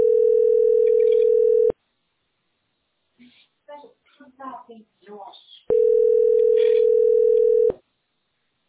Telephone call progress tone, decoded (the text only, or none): ringback tone